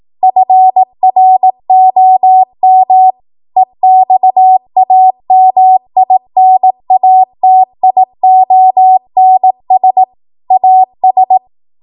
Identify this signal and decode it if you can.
Morse code, heard as a keyed tone, about 18 wpm, 750 Hz. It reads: FROM EXAMINATIONS AS